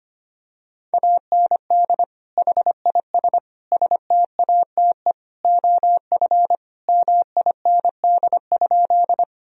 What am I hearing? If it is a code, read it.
Morse code, heard as a keyed tone, about 25 wpm, 700 Hz. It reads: AND HIS STATE OF MIND?
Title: Sentences from Top 1000 Words - Review Part 2 - 25wpm